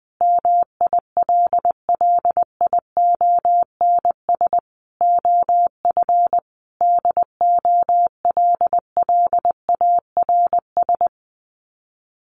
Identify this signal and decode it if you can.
Morse code, heard as a keyed tone, about 20 wpm, 700 Hz. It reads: MILLIONS OF DOLLARS
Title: Sets of 5 Words 20wpm